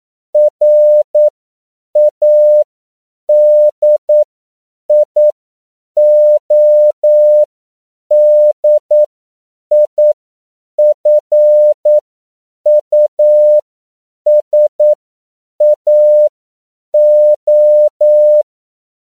Morse code, heard as a keyed tone, about 9 wpm, spaced as for 7 wpm, 600 Hz. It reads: RADIODIFUSAO
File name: CW / 10PPM